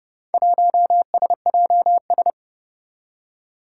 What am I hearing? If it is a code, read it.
Morse code, heard as a keyed tone, about 30 wpm, 700 Hz. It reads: 1SJS
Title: Four Challenging Letter-Number Combinations 30wpm